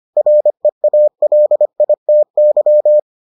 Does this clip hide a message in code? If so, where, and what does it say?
Yes: Morse code from 0.2 to 3.0 s, heard as a keyed tone, about 25 wpm, 600 Hz. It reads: REALITY